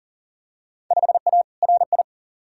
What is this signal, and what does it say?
Morse code, heard as a keyed tone, about 40 wpm, 700 Hz. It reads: 5A RI
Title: Field Day Exchanges Encore - 40wpm